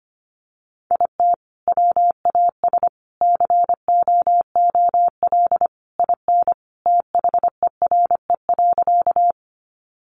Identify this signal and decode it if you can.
Morse code, heard as a keyed tone, about 25 wpm, 700 Hz. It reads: IT WAS COOL IN THERE.
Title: Sentences from Top 1200 Words 25wpm